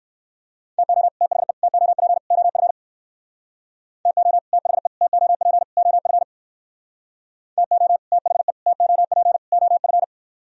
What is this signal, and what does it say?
Morse code, heard as a keyed tone, about 58 wpm, 700 Hz. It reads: TO THE TOP OF TO THE TOP OF TO THE TOP OF